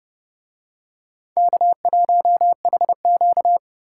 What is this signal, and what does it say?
Morse code, heard as a keyed tone, about 30 wpm, 700 Hz. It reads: K1HQ